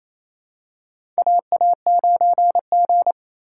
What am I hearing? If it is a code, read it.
Morse code, heard as a keyed tone, about 27 wpm, 700 Hz. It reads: AA9G
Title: USA Call Signs - 4 Characters - 4x Character Spacing - 28wpm